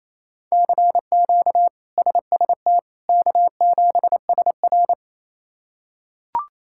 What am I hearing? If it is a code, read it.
Morse code, heard as a keyed tone, about 28 wpm, 700 Hz. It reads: CQ SST K7SR
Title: K1USN SST QSO Practice - 28wpm